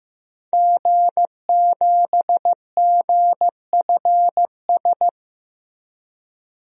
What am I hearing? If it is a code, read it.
Morse code, heard as a keyed tone, about 15 wpm, 700 Hz. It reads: G7GFS
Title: DXCC-100 Call Signs - 5 Characters - Repeated 2x - 15wpm